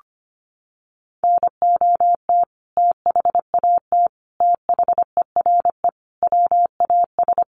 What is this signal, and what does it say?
Morse code, heard as a keyed tone, about 25 wpm, 700 Hz. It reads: NOT THAT THERE WAS
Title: Sentences from Top 1400 Words 25wpm